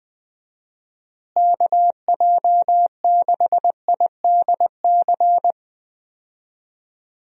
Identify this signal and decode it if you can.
Morse code, heard as a keyed tone, about 20 wpm, 700 Hz. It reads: KJ6IDC
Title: USA Call Signs - 6 Characters - Repeated 2x - 20wpm